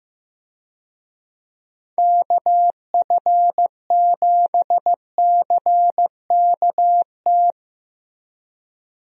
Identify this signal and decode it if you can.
Morse code, heard as a keyed tone, about 16 wpm, 700 Hz. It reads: KF7CKT
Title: USA Call Signs - 6 Characters - Repeated 2x - 15wpm